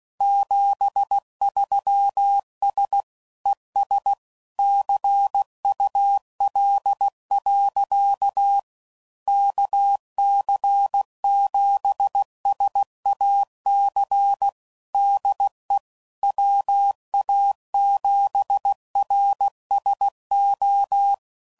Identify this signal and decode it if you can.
Morse code, heard as a keyed tone, about 16 wpm, 800 Hz. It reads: 73S ES CUL. KC7SAC DE WA7RSO
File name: CW Code Groups @15WPM / QSOs / kc7sac